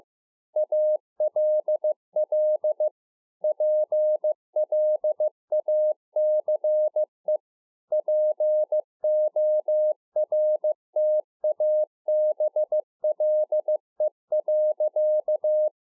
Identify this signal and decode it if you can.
Morse code, heard as a keyed tone, about 15 wpm, 610 Hz. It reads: ALL PLACE PORTABLE.